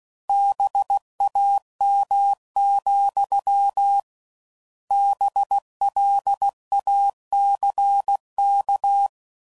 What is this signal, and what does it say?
Morse code, heard as a keyed tone, about 16 wpm, 800 Hz. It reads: BAM, BLACK